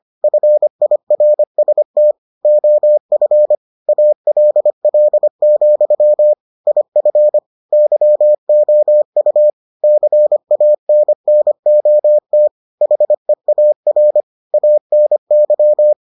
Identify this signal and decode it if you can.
Morse code, heard as a keyed tone, about 25 wpm, 600 Hz. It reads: FIRST OF ALL, IF YOU CANNOT HEAR ANY